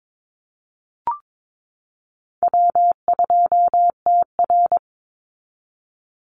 Morse code, heard as a keyed tone, about 22 wpm, 700 Hz. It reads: W2TR
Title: Easy Call Signs Repeated 3x - 22wpm